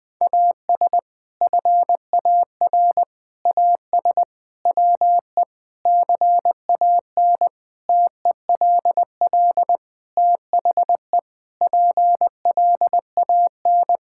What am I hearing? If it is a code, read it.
Morse code, heard as a keyed tone, about 21 wpm, 700 Hz. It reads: AS FAR AS WE CAN TELL THE PLAN
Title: Sentences with 6-Word Phrases - Spoken Beginning - 20wpm